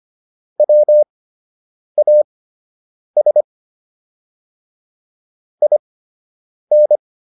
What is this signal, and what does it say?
Morse code, heard as a keyed tone, about 25 wpm, spaced as for 8 wpm, 600 Hz. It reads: WAS IN